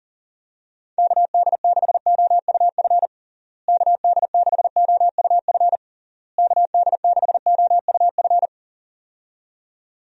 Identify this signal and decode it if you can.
Morse code, heard as a keyed tone, about 40 wpm, 700 Hz. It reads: KD6OUF KD6OUF KD6OUF
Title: USA Call Signs - 6 Characters - Repeated 3x - 40wpm